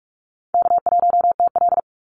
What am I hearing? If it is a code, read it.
Morse code, heard as a keyed tone, about 46 wpm, 700 Hz. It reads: K1TL
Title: Call Signs - 45wpm